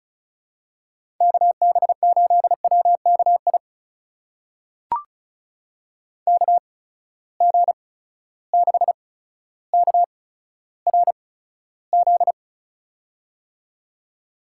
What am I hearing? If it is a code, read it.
Morse code, heard as a keyed tone, about 36 wpm, spaced as for 15 wpm, 700 Hz. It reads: KB8WKI KG6KRZ
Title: USA Call Signs - 6 Characters - 8x Character Spacing - 35wpm